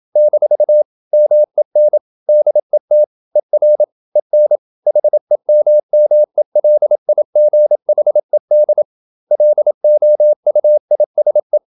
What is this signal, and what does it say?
Morse code, heard as a keyed tone, about 27 wpm, 600 Hz. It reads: - MEN DET ER EN HEMMELIGHED LOUISE
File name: Ferie QRQ 27wpm